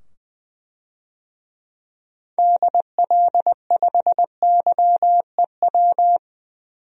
Morse code, heard as a keyed tone, about 21 wpm, 700 Hz. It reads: DL5YEW
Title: DXCC-100 Call Signs - 6 Characters - 20wpm